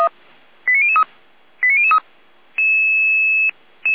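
Five-tone selective call (ZVEI), heard as quick runs of five tones, tones of about 70 ms. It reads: 89003 89003
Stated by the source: ZVEI five-tone call received via SDR by Feuerwehr Orlamünde.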